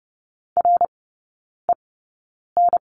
Morse code, heard as a keyed tone, about 30 wpm, spaced as for 9 wpm, 700 Hz. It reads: REN
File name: course / 30:10wpm / 087 Lesson 15e - Number 5 - Three Characters 30wpm